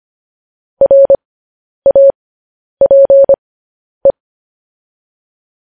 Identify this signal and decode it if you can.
Morse code, heard as a keyed tone, about 25 wpm, spaced as for 10 wpm, 550 Hz. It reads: RAPE
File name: podcast / News headlines-1x-25x10